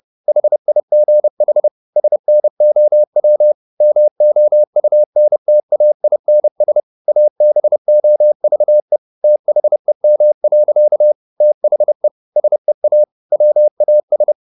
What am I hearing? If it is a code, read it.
Morse code, heard as a keyed tone, about 30 wpm, 600 Hz. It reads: HIGH SNOW MOUNTAINS ABOVE THEM. THE SEA WAS
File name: the old man and the sea / the old man and the sea 1 30wpm